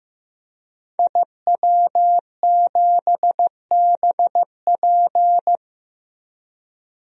Morse code, heard as a keyed tone, about 15 wpm, 700 Hz. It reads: IW7BP
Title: DXCC-100 Call Signs - 5 Characters - Repeated 3x - 15wpm